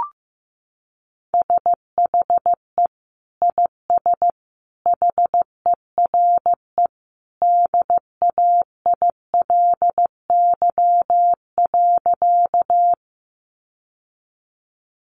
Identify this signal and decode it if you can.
Morse code, heard as a keyed tone, about 15 wpm, 700 Hz. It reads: SHE IS HERE DAILY.
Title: Sentences from Top 1100 Words 15wpm